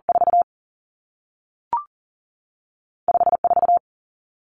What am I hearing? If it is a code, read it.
Morse code, heard as a keyed tone, about 38 wpm, 700 Hz. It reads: 4 5 4
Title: Two Challenging Letter-Number Combinations 40wpm